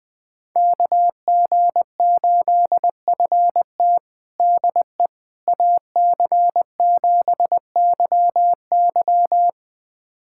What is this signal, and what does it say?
Morse code, heard as a keyed tone, about 21 wpm, 700 Hz. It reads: KG8FT DE AC7YY
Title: USA Callsign DE Callsign - 5 Characters - 20wpm